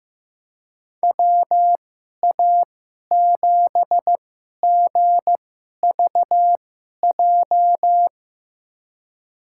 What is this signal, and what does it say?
Morse code, heard as a keyed tone, about 15 wpm, spaced as for 11 wpm, 700 Hz. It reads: WA7GVJ
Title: USA Call Signs - 2x Character Spacing - 15wpm